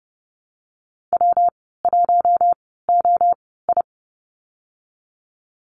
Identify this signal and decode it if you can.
Morse code, heard as a keyed tone, about 30 wpm, spaced as for 17 wpm, 700 Hz. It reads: W1OI
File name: Lesson 37g - Letter x - Call Signs 30wpm